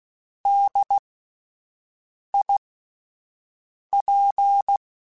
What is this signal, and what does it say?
Morse code, heard as a keyed tone, about 16 wpm, spaced as for 5 wpm, 800 Hz. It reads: DIP
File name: Spaced Alphas / Spaced Alphas mp3s / Ds sp @10